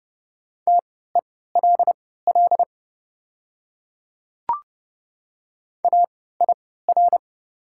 Morse code, heard as a keyed tone, about 31 wpm, spaced as for 17 wpm, 700 Hz. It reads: TELL AIR
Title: 3x Character Spacing - Top 200 Words - 30wpm